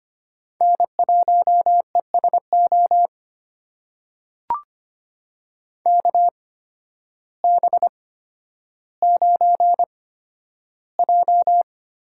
Morse code, heard as a keyed tone, about 25 wpm, spaced as for 12 wpm, 700 Hz. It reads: N1ESO KB9J
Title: USA Call Signs - 8x Character Spacing - 25wpm